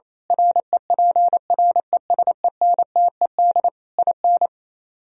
Morse code, heard as a keyed tone, about 26 wpm, 700 Hz. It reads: REPRESENTED IN
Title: Speed-Racing - Sets of 5 Words 28wpm